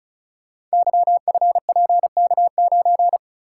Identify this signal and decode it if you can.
Morse code, heard as a keyed tone, about 36 wpm, 700 Hz. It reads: YFPK9
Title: Code Groups - 5 Letter:Numbers - 35wpm